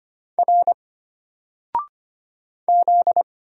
Morse code, heard as a keyed tone, about 25 wpm, 700 Hz. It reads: R Z
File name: Single-Letter-Number-25wpm